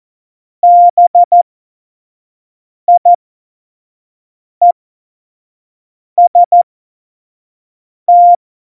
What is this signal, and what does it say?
Morse code, heard as a keyed tone, about 14 wpm, spaced as for 5 wpm, 700 Hz. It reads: BIEST
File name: Morsekurs / Witze / Witz 27. Oktober 2020, 25 F70